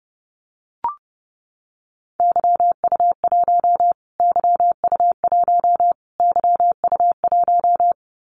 Morse code, heard as a keyed tone, about 30 wpm, 700 Hz. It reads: YU1 YU1 YU1
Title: Callsign Prefixes for Top 25 DXCC - Repeated 3x - 30wpm